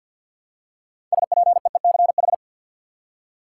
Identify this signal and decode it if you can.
Morse code, heard as a keyed tone, about 52 wpm, 700 Hz. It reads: SPEECH
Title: Six Letter Words - 50wpm